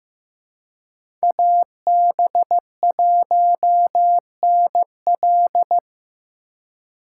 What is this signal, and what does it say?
Morse code, heard as a keyed tone, about 15 wpm, 700 Hz. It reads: AB1NL
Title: USA Call Signs - 6x Character Spacing - 15wpm